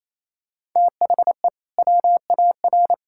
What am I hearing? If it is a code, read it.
Morse code, heard as a keyed tone, about 28 wpm, 700 Hz. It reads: THE WAR
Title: Sentences from Top 1000 Words - Review Part 5 - 28wpm